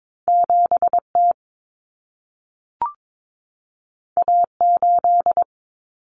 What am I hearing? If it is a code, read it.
Morse code, heard as a keyed tone, about 22 wpm, 700 Hz. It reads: 7T A8
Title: Common Cut Numbers - 2-Digit 22wpm